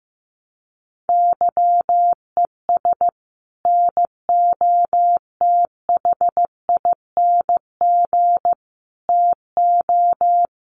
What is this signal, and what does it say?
Morse code, heard as a keyed tone, about 15 wpm, 700 Hz. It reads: YES NOTHING TO
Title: Sentences from Top 1300 Words 15wpm